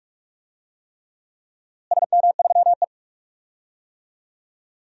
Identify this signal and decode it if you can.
Morse code, heard as a keyed tone, about 46 wpm, 700 Hz. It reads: SM3E
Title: DXCC-100 Call Signs - 4 Characters - Repeated 3x - 45wpm